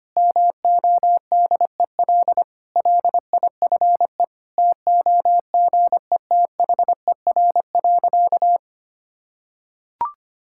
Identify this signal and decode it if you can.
Morse code, heard as a keyed tone, about 25 wpm, 700 Hz. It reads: MODEL LIFE TOGETHER.
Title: Sentences from Top 1000 Words - Review Part 5 - 25wpm